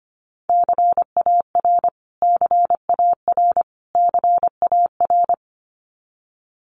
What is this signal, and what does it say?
Morse code, heard as a keyed tone, about 25 wpm, 700 Hz. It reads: CAR CAR CAR
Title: Top 500 Words Repeated 25wpm